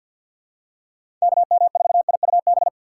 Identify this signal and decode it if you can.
Morse code, heard as a keyed tone, about 53 wpm, 700 Hz. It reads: KM4IUB
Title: USA Call Signs - 6 Characters - 4x Character Spacing - 50wpm